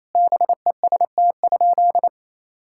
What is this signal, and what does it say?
Morse code, heard as a keyed tone, about 28 wpm, 700 Hz. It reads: BEST?
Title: Sentences from Top 1000 Words - Review Part 5 - 28wpm